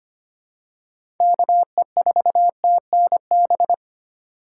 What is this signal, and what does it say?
Morse code, heard as a keyed tone, about 25 wpm, 700 Hz. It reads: KE4TNB
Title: USA Call Signs - 6 Characters - 2x Character Spacing - 25wpm